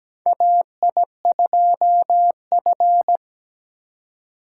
Morse code, heard as a keyed tone, about 17 wpm, 700 Hz. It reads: AI2F
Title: USA Call Signs - 4 Characters - 3x Character Spacing - 17wpm